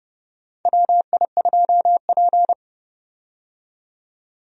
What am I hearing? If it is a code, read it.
Morse code, heard as a keyed tone, about 30 wpm, 700 Hz. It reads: WI2P
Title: USA Call Signs - 4 Characters - 30wpm